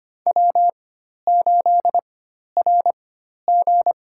Morse code, heard as a keyed tone, about 25 wpm, spaced as for 12 wpm, 700 Hz. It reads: W8RG
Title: USA Call Signs - 4 Characters - 4x Character Spacing - 25wpm